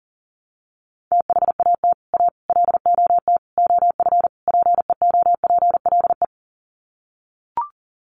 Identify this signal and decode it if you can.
Morse code, heard as a keyed tone, about 40 wpm, 700 Hz. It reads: THAT A LOT OF PEOPLE
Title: Sets of 5 Words 40wpm